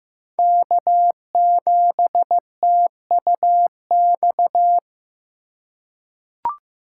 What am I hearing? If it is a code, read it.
Morse code, heard as a keyed tone, about 15 wpm, 700 Hz. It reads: K7TUX
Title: USA Call Signs - 5 Characters - 4x Character Spacing - 15wpm